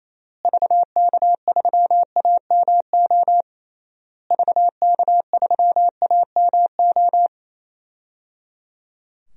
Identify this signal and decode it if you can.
Morse code, heard as a keyed tone, about 28 wpm, 700 Hz. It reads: VK3AMO VK3AMO
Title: DXCC-100 Call Signs - 6 Characters - Repeated 3x - 28wpm